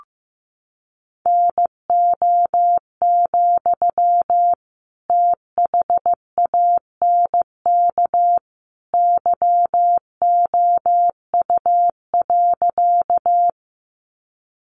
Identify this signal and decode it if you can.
Morse code, heard as a keyed tone, about 15 wpm, 700 Hz. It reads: NO, THANK YOU.
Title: Sentences from Top 600 Words 15wpm